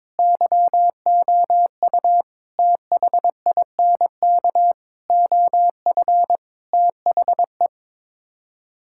Morse code, heard as a keyed tone, about 22 wpm, 700 Hz. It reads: YOU THINK OF THE
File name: practice / 22wpm / Sets of 6 Words - 22wpm